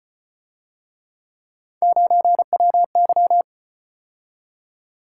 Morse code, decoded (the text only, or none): 9WY